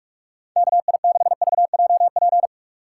Morse code, heard as a keyed tone, about 45 wpm, 700 Hz. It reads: KI6VJP
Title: USA Call Signs - 6 Characters - 6x Character Spacing - 45wpm